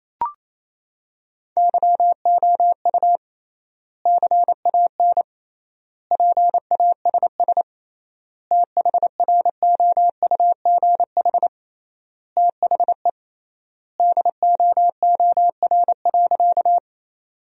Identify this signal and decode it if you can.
Morse code, heard as a keyed tone, about 28 wpm, 700 Hz. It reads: YOU CAN PASS THROUGH THE DOOR.
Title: 3x Word-Spacing - Sentences from Top 400 Words - 28wpm